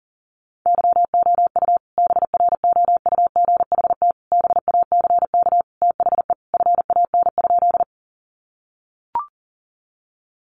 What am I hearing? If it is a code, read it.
Morse code, heard as a keyed tone, about 40 wpm, 700 Hz. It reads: YOU BROUGHT BACK THE FAN?
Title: Sentences from Top 1300 Words 40wpm